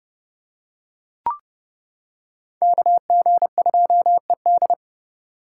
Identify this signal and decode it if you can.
Morse code, heard as a keyed tone, about 29 wpm, 700 Hz. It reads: KG2ED